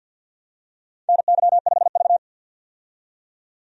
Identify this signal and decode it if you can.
Morse code, heard as a keyed tone, about 51 wpm, 700 Hz. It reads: NY5V
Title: USA Call Signs - 4 Characters - 50wpm